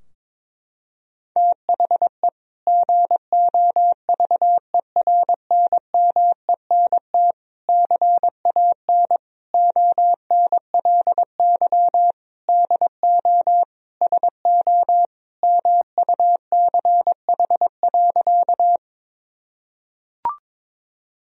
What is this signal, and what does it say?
Morse code, heard as a keyed tone, about 22 wpm, 700 Hz. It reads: THE GOVERNMENT CAN ONLY DO SO MUCH.
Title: Sentences from Top 1000 Words - Review Part 3 - 22wpm